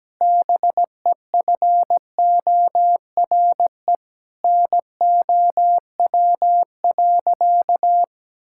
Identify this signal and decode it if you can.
Morse code, heard as a keyed tone, about 17 wpm, 700 Hz. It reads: BEFORE NOW.
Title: Sentences from Top 1000 Words - Review Part 6 - 17wpm